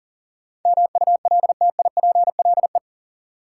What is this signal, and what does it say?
Morse code, heard as a keyed tone, about 40 wpm, 700 Hz. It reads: MULTIPLE
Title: Sets of 5 Words - New Word Drill - 40wpm